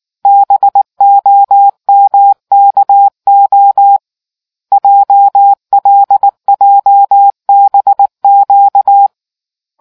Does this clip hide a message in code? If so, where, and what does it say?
Yes: Morse code from 0.3 to 9.1 s, heard as a keyed tone, about 19 wpm, 800 Hz. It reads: BOMKO JLJBQ